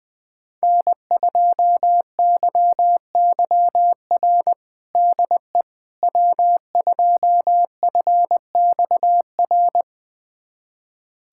Morse code, heard as a keyed tone, about 20 wpm, 700 Hz. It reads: N2YYR DE W2FXR
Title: USA Callsign DE Callsign - 5 Characters - 20wpm